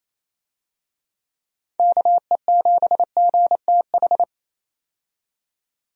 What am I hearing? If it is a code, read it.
Morse code, heard as a keyed tone, about 28 wpm, 700 Hz. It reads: KE7GTH
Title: USA Call Signs - 6 Characters - 28wpm